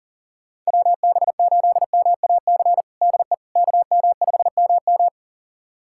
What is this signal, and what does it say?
Morse code, heard as a keyed tone, about 40 wpm, 700 Hz. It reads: WB8MAC DE KM5MM
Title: USA Callsign DE Callsign 40wpm